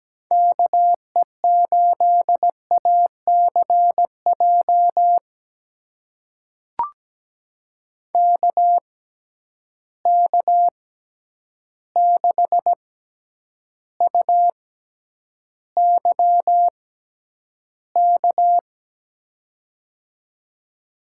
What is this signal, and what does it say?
Morse code, heard as a keyed tone, about 17 wpm, spaced as for 9 wpm, 700 Hz. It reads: KE8ACJ KK6UYK
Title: USA Call Signs - 6x Character Spacing - 17wpm